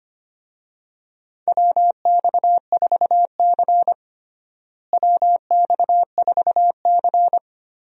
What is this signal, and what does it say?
Morse code, heard as a keyed tone, about 25 wpm, 700 Hz. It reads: WX4C WX4C
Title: USA Call Signs - 4 Characters - Repeated 2x - 25wpm